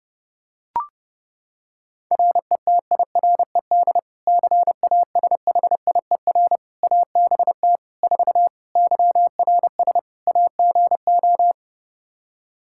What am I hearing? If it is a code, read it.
Morse code, heard as a keyed tone, about 30 wpm, 700 Hz. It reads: RETIRED CASHIER ABT 4 YRS AGO
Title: Practice QSO - Exchange 3 - More Info - 30wpm